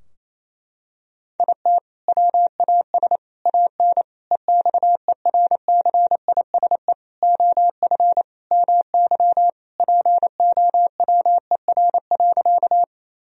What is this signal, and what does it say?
Morse code, heard as a keyed tone, about 28 wpm, 700 Hz. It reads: IT WAS AN EXERCISE OF MY POWER.
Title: Sentences from Top 1000 Words - Review Part 4 - 28wpm